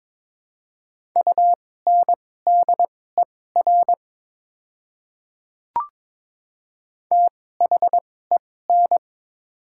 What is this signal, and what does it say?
Morse code, heard as a keyed tone, about 23 wpm, spaced as for 16 wpm, 700 Hz. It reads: UNDER THEN
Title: 2x Character Spacing - Top 200 Words - 22wpm